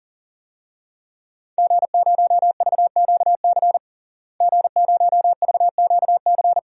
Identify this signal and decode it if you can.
Morse code, heard as a keyed tone, about 40 wpm, 700 Hz. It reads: G0VQC G0VQC
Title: DXCC-100 Call Signs - 5 Characters - Repeated 2x - 40wpm